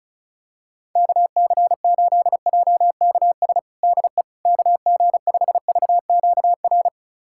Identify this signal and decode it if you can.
Morse code, heard as a keyed tone, about 36 wpm, 700 Hz. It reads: KC8JKS DE KG5VQR
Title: USA Callsign DE Callsign - 6 Characters - 35wpm